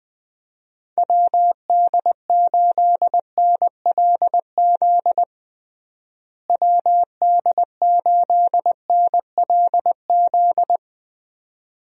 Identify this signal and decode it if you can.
Morse code, heard as a keyed tone, about 20 wpm, 700 Hz. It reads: WD8NLZ WD8NLZ
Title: USA Call Signs - 6 Characters - Repeated 2x - 20wpm